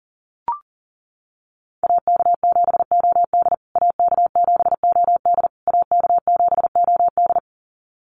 Morse code, heard as a keyed tone, about 40 wpm, 700 Hz. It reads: AK7OD AK7OD AK7OD
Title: USA Call Signs Repeated 3x 40wpm